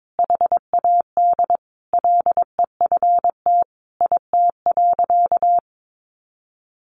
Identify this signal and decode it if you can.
Morse code, heard as a keyed tone, about 22 wpm, 700 Hz. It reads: HAD LEFT IT.
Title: A Sound Of Thunder 22wpm